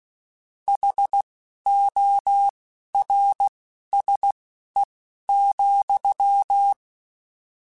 Morse code, heard as a keyed tone, about 16 wpm, spaced as for 11 wpm, 800 Hz. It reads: HORSE,
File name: Misc / Misc-mp3s / Word Pairs @10